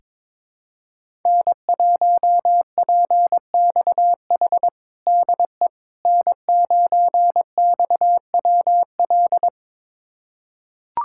Morse code, heard as a keyed tone, about 22 wpm, 700 Hz. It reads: N1PXH DE N9XWL
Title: USA Callsign DE Callsign - 5 Characters - 22wpm